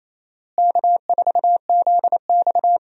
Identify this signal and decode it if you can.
Morse code, heard as a keyed tone, about 27 wpm, 700 Hz. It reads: K4ZX
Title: USA Call Signs - 4 Characters - Repeated 3x - 28wpm